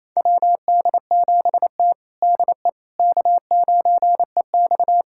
Morse code, heard as a keyed tone, about 28 wpm, 690 Hz. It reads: WD7T DE K9EX
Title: USA Callsign DE Callsign - 4 Characters - 28wpm